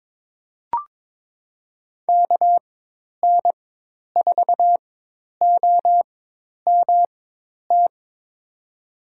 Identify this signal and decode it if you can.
Morse code, heard as a keyed tone, about 22 wpm, spaced as for 10 wpm, 700 Hz. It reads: KN4OMT
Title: USA Call Signs - 6 Characters - 4x Character Spacing - 22wpm